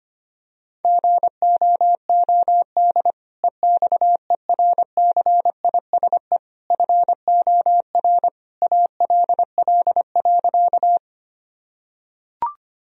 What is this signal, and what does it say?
Morse code, heard as a keyed tone, about 25 wpm, 700 Hz. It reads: GOOD EXERCISE FOR ALL.
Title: Sentences from Top 1000 Words - Review Part 4 - 25wpm